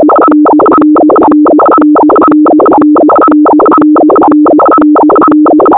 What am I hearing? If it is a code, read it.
Two-tone teleprinter signal, about 200 Hz shift.
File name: thet / radia297 / radia s24 n297 radiohelsinki-raggam--dataradio-pt4